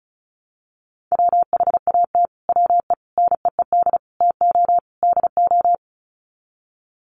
Morse code, heard as a keyed tone, about 34 wpm, 700 Hz. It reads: WHAT WE NEED TO DO
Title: Sets of 5 Words Encore 35wpm